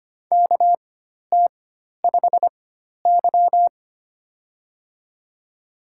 Morse code, heard as a keyed tone, about 25 wpm, spaced as for 12 wpm, 700 Hz. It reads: KT5Y
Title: USA Call Signs - 4 Characters - 4x Character Spacing - 25wpm